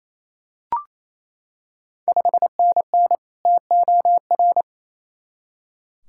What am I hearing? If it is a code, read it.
Morse code, heard as a keyed tone, about 27 wpm, 700 Hz. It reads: 5NN TOR
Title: Ontario QSO Party Exchange 28wpm